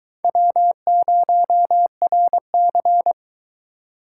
Morse code, heard as a keyed tone, about 22 wpm, 700 Hz. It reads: W0RC